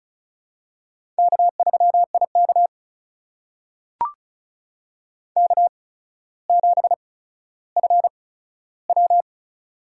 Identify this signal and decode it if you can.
Morse code, heard as a keyed tone, about 36 wpm, spaced as for 15 wpm, 700 Hz. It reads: K3IK K7FW
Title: USA Call Signs - 4 Characters - 8x Character Spacing - 35wpm